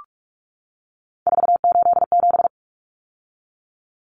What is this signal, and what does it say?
Morse code, heard as a keyed tone, about 44 wpm, 700 Hz. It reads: 487
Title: Common Cut Numbers - 3-Digit 45wpm